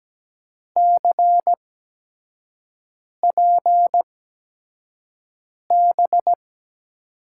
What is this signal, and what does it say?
Morse code, heard as a keyed tone, about 17 wpm, spaced as for 5 wpm, 700 Hz. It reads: CPB
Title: USA Call Signs - 8x Character Spacing - 17wpm